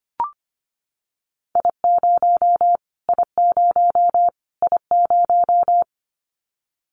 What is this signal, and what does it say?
Morse code, heard as a keyed tone, about 25 wpm, 700 Hz. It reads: I0 I0 I0
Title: Callsign Prefixes for Top 25 DXCC - Repeated 3x - 25wpm